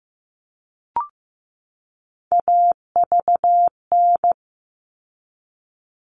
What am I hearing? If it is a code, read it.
Morse code, heard as a keyed tone, about 15 wpm, 700 Hz. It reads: AVN
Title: All Cut Numbers - 3-Digit 15wpm